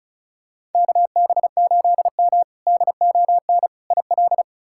Morse code, heard as a keyed tone, about 34 wpm, 700 Hz. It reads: KB8M DON IL
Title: K1USN SST QSO Practice - 35wpm